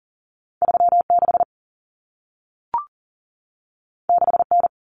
Morse code, heard as a keyed tone, about 40 wpm, 700 Hz. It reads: 36 6N
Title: Common Cut Numbers - 2-Digit 40wpm